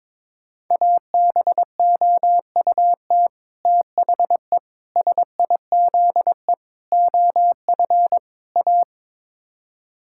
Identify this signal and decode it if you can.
Morse code, heard as a keyed tone, about 21 wpm, 700 Hz. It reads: ABOUT THE SIZE OF A
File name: practice / 22wpm / Build A Phrase - 5-Word Phrases - 22wpm